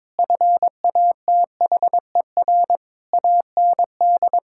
Morse code, heard as a keyed tone, about 21 wpm, 700 Hz. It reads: FATHER AND
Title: Sentences from Top 1600 Words 22wpm